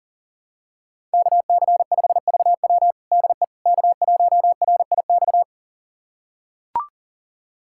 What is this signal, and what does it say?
Morse code, heard as a keyed tone, about 40 wpm, 700 Hz. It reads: KC5VW DE K1RIX
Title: USA Callsign DE Callsign 40wpm